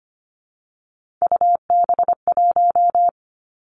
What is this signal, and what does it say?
Morse code, heard as a keyed tone, about 25 wpm, 700 Hz. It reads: UB1
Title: Three Challenging Letter-Number Combinations 25wpm